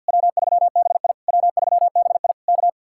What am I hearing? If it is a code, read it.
Morse code, heard as a keyed tone, about 66 wpm, spaced as for 58 wpm, 700 Hz. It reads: W3BI W3BI K
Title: Speed-Racing Common QSO Phrases Encore - 50wpm